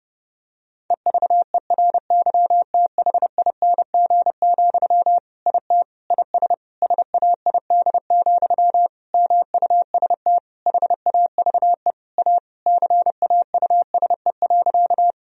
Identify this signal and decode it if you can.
Morse code, heard as a keyed tone, about 30 wpm, 700 Hz. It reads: EVERYTHING, IT IS SAID, MUST HAVE A CAUSE.
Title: Sentences from Top 1000 Words - Review Part 1 - 30wpm